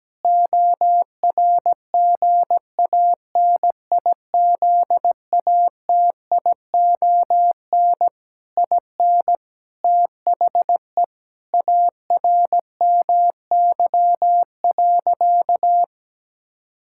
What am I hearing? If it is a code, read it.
Morse code, heard as a keyed tone, about 17 wpm, 700 Hz. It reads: ORGANIZATION IN THE ARMY.
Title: Sentences from Top 1000 Words - Review Part 5 - 17wpm